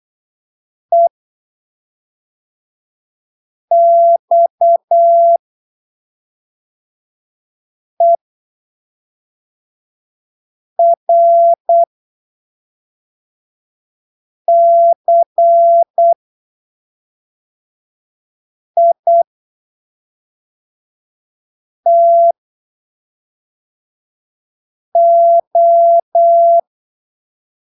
Morse code, heard as a keyed tone, about 8 wpm, spaced as for 3 wpm, 680 Hz. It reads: EXERCITO
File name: arquivos / texto / texto 14 08ppm